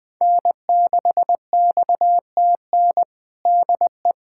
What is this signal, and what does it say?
Morse code, heard as a keyed tone, about 21 wpm, 700 Hz. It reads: N6XTN DE
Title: USA Callsign DE Callsign 20wpm